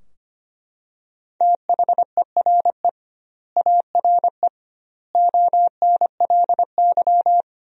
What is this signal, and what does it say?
Morse code, heard as a keyed tone, about 25 wpm, 700 Hz. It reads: THERE ARE ONLY
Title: 2x Word-Spacing - Sentences from Top 200 Words - 25wpm